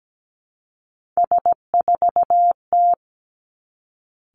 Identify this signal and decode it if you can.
Morse code, heard as a keyed tone, about 17 wpm, 700 Hz. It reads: S4T
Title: Three Challenging Letter-Number Combinations 17wpm